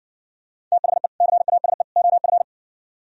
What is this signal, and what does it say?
Morse code, heard as a keyed tone, about 60 wpm, 700 Hz. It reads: THE CASE OF